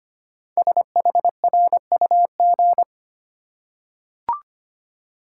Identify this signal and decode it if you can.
Morse code, heard as a keyed tone, about 25 wpm, 700 Hz. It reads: SHRUG
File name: 5 Letter Words - First 3 Letters Given - 25wpm